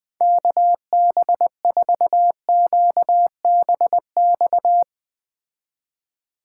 Morse code, heard as a keyed tone, about 20 wpm, 700 Hz. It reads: KB4QBX
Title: Speed Racing - USA 2x3 Call Signs 17wpm